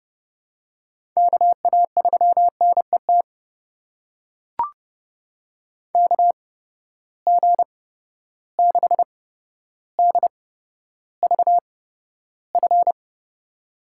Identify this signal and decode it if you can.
Morse code, heard as a keyed tone, about 30 wpm, spaced as for 13 wpm, 700 Hz. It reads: KA3NET KG6DVF
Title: USA Call Signs - 6 Characters - 8x Character Spacing - 30wpm